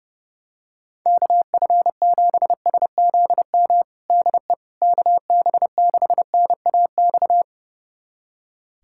Morse code, heard as a keyed tone, about 30 wpm, 700 Hz. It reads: KF7SZM DE KB6NAX
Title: USA Callsign DE Callsign - 6 Characters - 30wpm